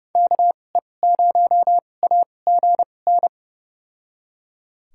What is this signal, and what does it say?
Morse code, heard as a keyed tone, about 30 wpm, spaced as for 22 wpm, 700 Hz. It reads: KE0AGN
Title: USA Call Signs - 6 Characters - 2x Character Spacing - 30wpm